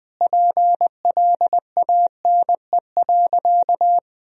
Morse code, heard as a keyed tone, about 21 wpm, 700 Hz. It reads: PLANE.
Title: Sentences from Top 1000 Words - Review Part 3 - 20wpm